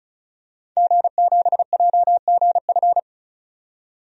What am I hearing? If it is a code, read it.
Morse code, heard as a keyed tone, about 36 wpm, 700 Hz. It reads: G7JGF
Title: DXCC-100 Call Signs - 5 Characters - 35wpm